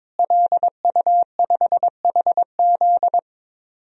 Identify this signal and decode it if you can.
Morse code, heard as a keyed tone, about 22 wpm, 700 Hz. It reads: LU5HZ
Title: DXCC-100 Call Signs - 5 Characters - Repeated 2x - 22wpm